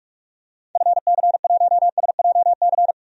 Morse code, heard as a keyed tone, about 46 wpm, 700 Hz. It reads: FC1SJC
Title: DXCC-100 Call Signs - 6 Characters - Repeated 3x - 45wpm